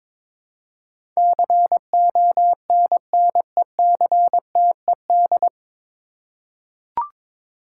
Morse code, heard as a keyed tone, about 22 wpm, 700 Hz. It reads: CONNECTED
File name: 9 Letter Words - First 3 Letters Given - 22wpm